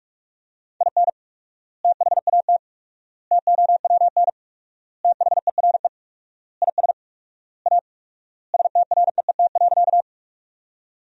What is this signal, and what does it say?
Morse code, heard as a keyed tone, about 47 wpm, 700 Hz. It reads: IN THAT TOWN THERE IS A STREET.
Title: 4x Word-Spacing - Sentences from Top 500 Words - 45wpm